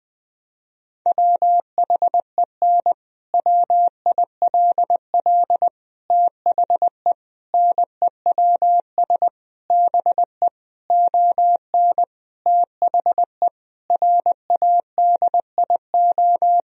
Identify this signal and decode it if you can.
Morse code, heard as a keyed tone, about 20 wpm, 700 Hz. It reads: WHEN WILL THE NEWS BE ON THE RADIO
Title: Sentences from Top 1000 Words - Review Part 3 - 20wpm